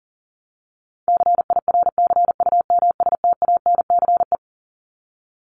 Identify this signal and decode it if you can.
Morse code, heard as a keyed tone, about 40 wpm, 700 Hz. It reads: CIRCUMSTANCE
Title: ES:S Suffix - Top 100 Words with Base Words 40wpm